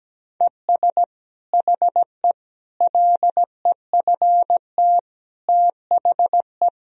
Morse code, heard as a keyed tone, about 17 wpm, 700 Hz. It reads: ES HE LEFT THE
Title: Sentences from Top 300 Words and Abbreviations 17wpm